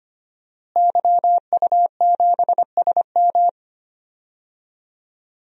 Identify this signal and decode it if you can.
Morse code, heard as a keyed tone, about 25 wpm, 700 Hz. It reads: YU7SM
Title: DXCC-100 Call Signs - 5 Characters - Repeated 3x - 25wpm